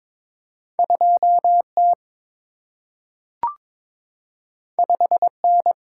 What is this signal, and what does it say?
Morse code, heard as a keyed tone, about 22 wpm, 700 Hz. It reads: 2 T 5 N
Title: Single-Letter Single-Number 22wpm